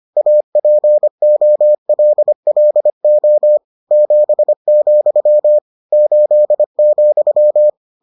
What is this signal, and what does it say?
Morse code, heard as a keyed tone, about 25 wpm, 600 Hz. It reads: APOLLO 7, 8,